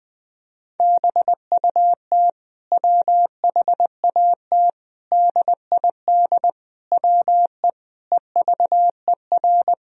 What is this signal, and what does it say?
Morse code, heard as a keyed tone, about 19 wpm, 700 Hz. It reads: BUT WHAT DID WE EVER
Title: Sentences from Top 400 Words and Abbreviations 20wpm